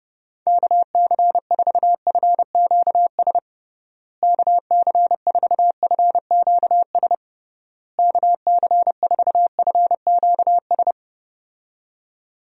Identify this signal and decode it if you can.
Morse code, heard as a keyed tone, about 32 wpm, 700 Hz. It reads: KC4FQS KC4FQS KC4FQS
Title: USA Call Signs - 6 Characters - Repeated 3x - 30wpm